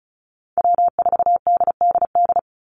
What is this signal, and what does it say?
Morse code, heard as a keyed tone, about 34 wpm, 700 Hz. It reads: W4DDD
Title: Call Signs Repeated 3x - 35wpm